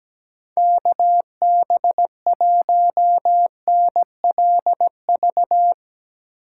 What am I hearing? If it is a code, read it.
Morse code, heard as a keyed tone, about 17 wpm, 700 Hz. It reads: KB1NLV
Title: Speed Racing - USA Call Signs 17wpm